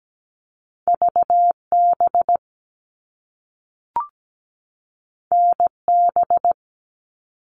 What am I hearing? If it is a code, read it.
Morse code, heard as a keyed tone, about 17 wpm, 700 Hz. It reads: VB NB
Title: All Cut Numbers - 2-Digit 17wpm